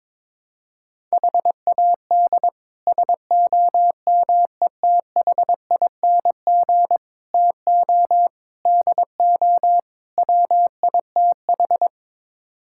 Morse code, heard as a keyed tone, about 22 wpm, 700 Hz. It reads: HAD SOMETHING TO DO WITH
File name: practice / 22wpm / Build A Phrase - 5-Word Phrases - 22wpm